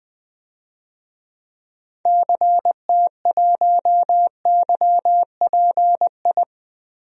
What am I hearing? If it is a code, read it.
Morse code, heard as a keyed tone, about 20 wpm, 700 Hz. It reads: CT1YPI
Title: DXCC-100 Call Signs - 6 Characters - Repeated 3x - 20wpm